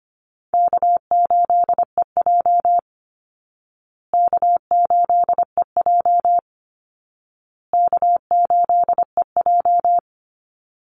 Morse code, heard as a keyed tone, about 25 wpm, 700 Hz. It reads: K8EJ K8EJ K8EJ
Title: Easy Call Signs Repeated 3x - 25wpm